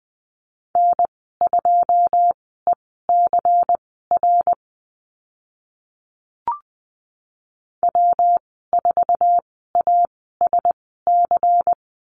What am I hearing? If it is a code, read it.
Morse code, heard as a keyed tone, about 20 wpm, spaced as for 15 wpm, 700 Hz. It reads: N2ECR W4ASC
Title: 122 Lesson 20g - Number 3 - Call Signs 20wpm